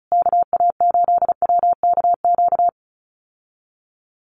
Morse code, heard as a keyed tone, about 34 wpm, 700 Hz. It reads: KA8WKQ
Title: USA Call Signs Repeated 2x 35wpm